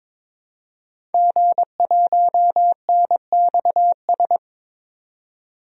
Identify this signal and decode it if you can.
Morse code, heard as a keyed tone, about 22 wpm, 700 Hz. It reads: G1NXS
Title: DXCC-100 Call Signs - 5 Characters - 22wpm